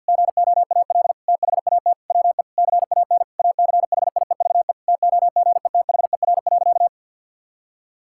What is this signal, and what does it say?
Morse code, heard as a keyed tone, about 50 wpm, 700 Hz. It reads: GOAL THAT WE CAN ACHIEVE TOGETHER.
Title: Sentences from Top 1000 Words - Review Part 6 - 50wpm